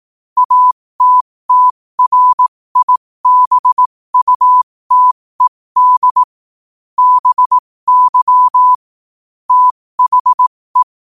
Morse code, heard as a keyed tone, about 18 wpm, spaced as for 15 wpm, 1000 Hz. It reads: ATTRIBUTED BY THE
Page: New quotes every day in morse code at 15 Words per minute.